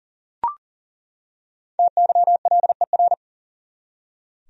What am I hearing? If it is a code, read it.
Morse code, heard as a keyed tone, about 40 wpm, 700 Hz. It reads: TYLER
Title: USA Common Names - 40wpm